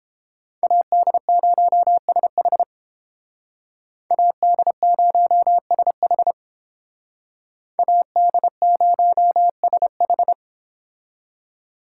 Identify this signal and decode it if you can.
Morse code, heard as a keyed tone, about 31 wpm, 700 Hz. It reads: AD0SH AD0SH AD0SH